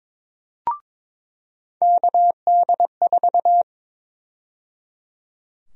Morse code, heard as a keyed tone, about 22 wpm, 700 Hz. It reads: KD4
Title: USA Call Sign Prefixes - 22wpm